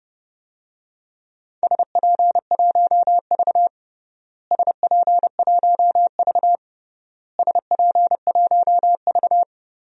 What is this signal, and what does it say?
Morse code, heard as a keyed tone, about 32 wpm, 700 Hz. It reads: SP1V SP1V SP1V
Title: DXCC-100 Call Signs - 4 Characters - Repeated 3x - 30wpm